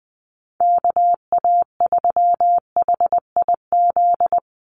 Morse code, heard as a keyed tone, about 20 wpm, 700 Hz. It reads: KA3HIZ
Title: USA Call Signs 20wpm